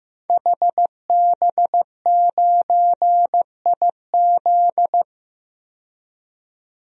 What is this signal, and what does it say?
Morse code, heard as a keyed tone, about 15 wpm, 700 Hz. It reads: HB9IZ
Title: DXCC-100 Call Signs - 5 Characters - 15wpm